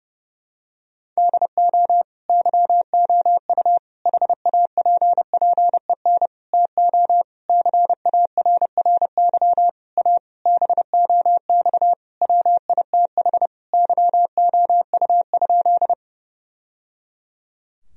Morse code, heard as a keyed tone, about 29 wpm, 700 Hz. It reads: DO YOU HAPPEN TO CARRY A BOX WITH YOU?
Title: Sentences from Top 1000 Words - Review Part 2 - 30wpm